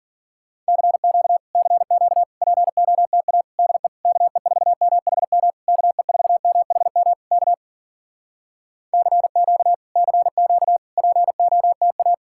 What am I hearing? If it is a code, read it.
Morse code, heard as a keyed tone, about 57 wpm, 700 Hz. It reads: CQ CQ POTA DE KE4MHM KE4MHM K CQ CQ POTA